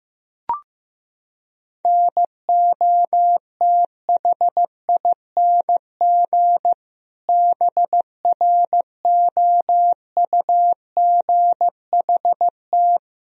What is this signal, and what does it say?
Morse code, heard as a keyed tone, about 15 wpm, 700 Hz. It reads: NOTHING BROUGHT
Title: Sentences from Top 1000 Words - Review Part 1 - 15wpm